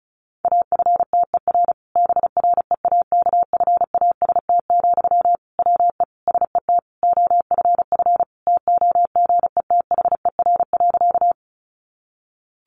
Morse code, heard as a keyed tone, about 34 wpm, 700 Hz. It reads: AFTER BREAKFAST, WE SET OFF TOGETHER.
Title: Sentences from Top 1500 Words 35wpm